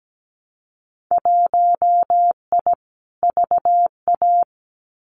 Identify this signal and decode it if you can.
Morse code, heard as a keyed tone, about 17 wpm, 700 Hz. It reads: 1I VA
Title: Winter Field Day Exchanges 17wpm